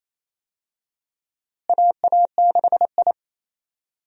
Morse code, heard as a keyed tone, about 28 wpm, 700 Hz. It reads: AA6I
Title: USA Call Signs - 4 Characters - Repeated 3x - 28wpm